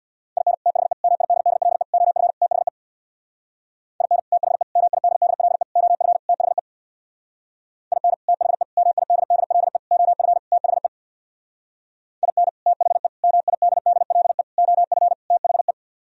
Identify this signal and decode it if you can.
Morse code, heard as a keyed tone, about 64 wpm, 700 Hz. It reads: IN THE MIDDLE OF THE IN THE MIDDLE OF THE IN THE MIDDLE OF THE IN THE MIDDLE OF THE